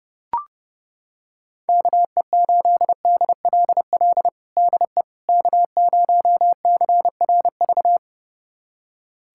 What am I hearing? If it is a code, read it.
Morse code, heard as a keyed tone, about 30 wpm, 700 Hz. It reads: KE8DLL DE K0CRV
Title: USA Callsign DE Callsign 30wpm